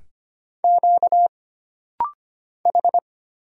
Morse code, heard as a keyed tone, about 25 wpm, 700 Hz. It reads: Q H
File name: Single Character (Letter:Number:Punctuation:Prosign) - Rapid-Fire - 25wpm